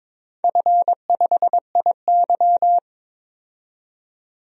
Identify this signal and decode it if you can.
Morse code, heard as a keyed tone, about 22 wpm, 700 Hz. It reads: F5IY
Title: DXCC-100 Call Signs - 4 Characters - 22wpm